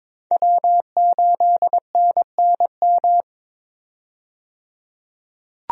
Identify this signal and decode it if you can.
Morse code, heard as a keyed tone, about 22 wpm, 700 Hz. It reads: W8NNM
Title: USA Call Signs - 5 Characters - 22wpm